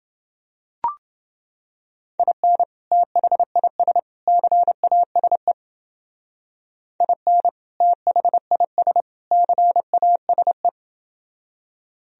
Morse code, heard as a keyed tone, about 30 wpm, 700 Hz. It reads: IN THIS CASE IN THIS CASE